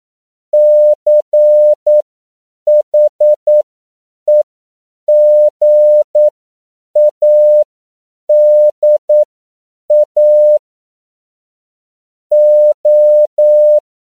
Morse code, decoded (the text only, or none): CHEGADA O